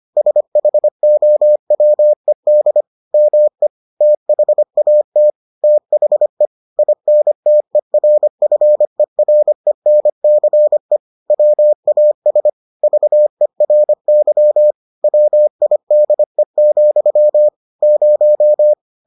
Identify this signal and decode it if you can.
Morse code, heard as a keyed tone, about 25 wpm, 600 Hz. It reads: SHOWED ME THAT THE INTERFERENCE WAS VERY WIDE, 0